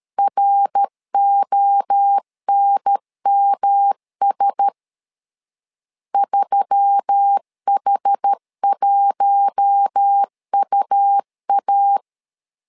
Morse code, heard as a keyed tone, about 13 wpm, 800 Hz. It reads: RONMS 3H1UA
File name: LEZIONI / 4 Q1S3 12wpm 2023